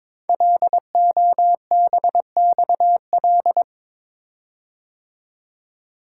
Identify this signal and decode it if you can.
Morse code, heard as a keyed tone, about 22 wpm, 700 Hz. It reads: LOBXL
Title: Code Groups - 5 Letters - 22wpm